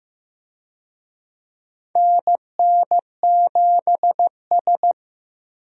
Morse code, heard as a keyed tone, about 15 wpm, 700 Hz. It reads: NN7S
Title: USA Call Signs - 4 Characters - Repeated 2x - 15wpm